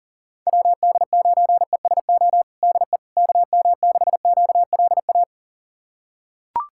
Morse code, heard as a keyed tone, about 40 wpm, 700 Hz. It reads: WD9ESO DE KM6QLA
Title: USA Callsign DE Callsign 40wpm